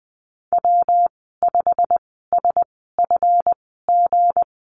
Morse code, heard as a keyed tone, about 20 wpm, spaced as for 14 wpm, 700 Hz. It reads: W5SFG
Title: 200 Lesson 32g - Letter k - Call Signs 20wpm